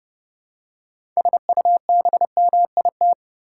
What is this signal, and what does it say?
Morse code, heard as a keyed tone, about 31 wpm, 700 Hz. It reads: SUBMIT
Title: Six Letter Words - 30wpm